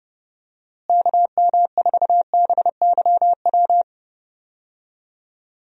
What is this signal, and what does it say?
Morse code, heard as a keyed tone, about 30 wpm, 700 Hz. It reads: KM4BYW
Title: USA Call Signs - 6 Characters - Repeated 3x - 30wpm